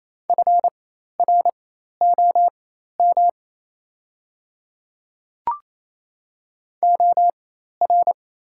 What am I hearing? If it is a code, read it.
Morse code, heard as a keyed tone, about 27 wpm, spaced as for 13 wpm, 700 Hz. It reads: FROM OR
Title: 4x Character Spacing - Top 200 Words - 28wpm